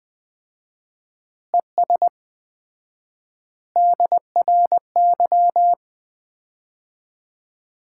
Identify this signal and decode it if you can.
Morse code, heard as a keyed tone, about 20 wpm, 700 Hz. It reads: ES DRY
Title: Common QSO Phrases - 4x Word-Spacing - 20wpm